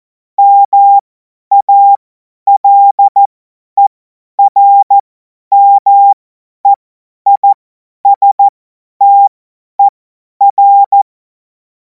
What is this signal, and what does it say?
Morse code, heard as a keyed tone, about 14 wpm, spaced as for 10 wpm, 800 Hz. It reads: MALERMEISTER
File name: Morsekurs / Witze / Witz 24. Februar 2026 50 F70